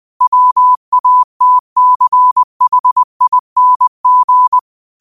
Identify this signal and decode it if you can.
Morse code, heard as a keyed tone, about 20 wpm, 1000 Hz. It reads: WATCHING